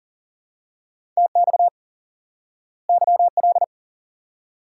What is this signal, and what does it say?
Morse code, heard as a keyed tone, about 40 wpm, 700 Hz. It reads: TX YL
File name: IQE Course 3 - Lesson 89 - YAGI - Two Elements - 40wpm